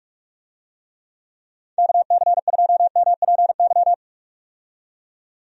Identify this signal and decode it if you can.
Morse code, heard as a keyed tone, about 43 wpm, 700 Hz. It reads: KC2MPY
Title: USA Call Signs - 6 Characters - 45wpm